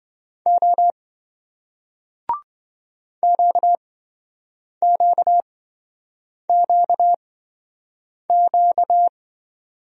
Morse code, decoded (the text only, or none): O QQQQ